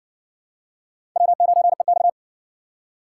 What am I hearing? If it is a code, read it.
Morse code, heard as a keyed tone, about 60 wpm, 700 Hz. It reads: W9EX